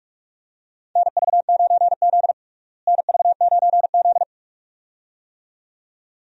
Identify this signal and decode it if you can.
Morse code, heard as a keyed tone, about 47 wpm, 700 Hz. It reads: NV9Z NV9Z
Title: USA Call Signs - 4 Characters - Repeated 2x - 45wpm